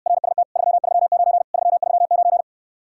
Morse code, heard as a keyed tone, about 69 wpm, 700 Hz. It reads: RST 339 339